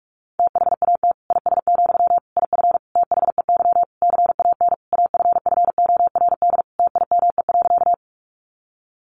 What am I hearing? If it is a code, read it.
Morse code, heard as a keyed tone, about 44 wpm, 700 Hz. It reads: THAT IS, IF THEY CAN AFFORD TIME.
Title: Sentences from Top 1100 Words 45wpm